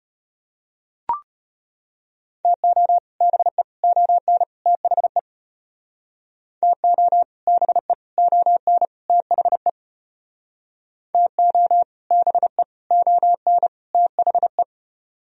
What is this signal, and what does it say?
Morse code, heard as a keyed tone, about 32 wpm, 700 Hz. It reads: TO BE ON THE TO BE ON THE TO BE ON THE